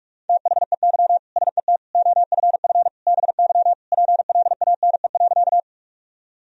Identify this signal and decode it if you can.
Morse code, heard as a keyed tone, about 46 wpm, 700 Hz. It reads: THEY SET OFF BY PLANE.
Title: Sentences from Top 1000 Words - Review Part 3 - 45wpm